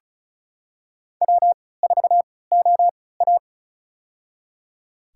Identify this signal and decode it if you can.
Morse code, heard as a keyed tone, about 34 wpm, spaced as for 20 wpm, 700 Hz. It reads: W4OA
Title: USA Call Signs - 4 Characters - 3x Character Spacing - 35wpm